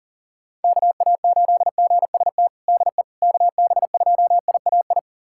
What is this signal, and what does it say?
Morse code, heard as a keyed tone, about 40 wpm, 700 Hz. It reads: KA8GST DE KB2IAI
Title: USA Callsign DE Callsign - 6 Characters - 40wpm